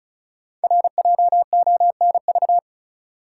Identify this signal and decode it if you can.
Morse code, heard as a keyed tone, about 36 wpm, 700 Hz. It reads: RJONV
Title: Code Groups - 5 Letters - 35wpm